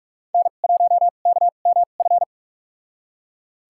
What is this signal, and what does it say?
Morse code, heard as a keyed tone, about 41 wpm, spaced as for 31 wpm, 700 Hz. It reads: N1KMF
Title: USA Call Signs - 2x Character Spacing - 45wpm